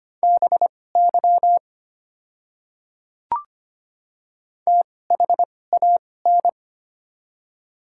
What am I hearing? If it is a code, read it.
Morse code, heard as a keyed tone, about 25 wpm, spaced as for 18 wpm, 700 Hz. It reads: BY THAN
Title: 2x Character Spacing - Top 300 Words - 25wpm